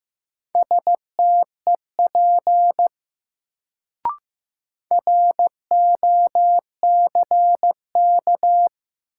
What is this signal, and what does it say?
Morse code, heard as a keyed tone, about 15 wpm, 700 Hz. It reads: STEP ROCK
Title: Sentences from Top 400 Words - New Word Drill - 15wpm